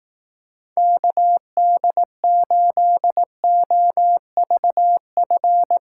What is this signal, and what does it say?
Morse code, heard as a keyed tone, about 18 wpm, 700 Hz. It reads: KD8OVF
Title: Speed Racing - USA 2x3 Call Signs 15wpm